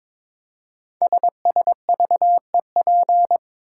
Morse code, heard as a keyed tone, about 22 wpm, 700 Hz. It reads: SSVEP
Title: Code Groups - 5 Letters - 22wpm